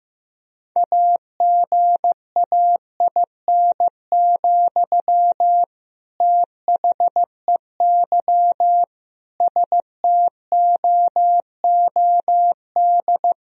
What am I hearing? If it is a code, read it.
Morse code, heard as a keyed tone, about 15 wpm, 700 Hz. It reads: AGAIN, THEY STOOD
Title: Sentences from Top 1000 Words - Review Part 2 - 15wpm